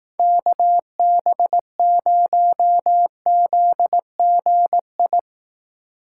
Morse code, heard as a keyed tone, about 18 wpm, 700 Hz. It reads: KB0ZGI
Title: Speed Racing - USA 2x3 Call Signs 15wpm